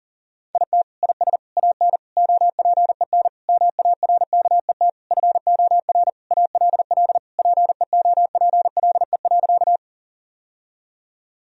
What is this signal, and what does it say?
Morse code, heard as a keyed tone, about 37 wpm, 700 Hz. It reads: IT IS AN OPEN MARKET FOR ALL PEOPLE.
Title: Sentences from Top 1000 Words - Review Part 3 - 40wpm